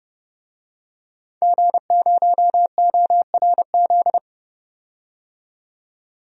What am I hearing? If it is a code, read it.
Morse code, heard as a keyed tone, about 30 wpm, 700 Hz. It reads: G0ORZ
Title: DXCC-100 Call Signs - 5 Characters - 30wpm